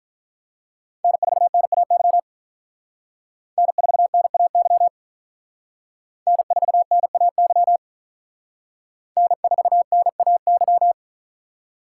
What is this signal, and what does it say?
Morse code, heard as a keyed tone, about 45 wpm, 700 Hz. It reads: N4NAY N4NAY N4NAY N4NAY